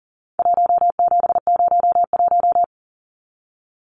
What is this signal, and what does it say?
Morse code, heard as a keyed tone, about 40 wpm, 700 Hz. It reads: 1701
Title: Numbers - 4-Digit 40wpm